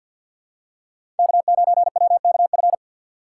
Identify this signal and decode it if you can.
Morse code, heard as a keyed tone, about 52 wpm, 690 Hz. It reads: K9WKF
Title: USA Call Signs - 5 Characters - 6x Character Spacing - 50wpm